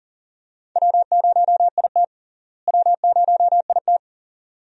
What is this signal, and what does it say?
Morse code, heard as a keyed tone, about 40 wpm, 700 Hz. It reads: W0IT W0IT
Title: USA Call Signs - 4 Characters - Repeated 2x - 40wpm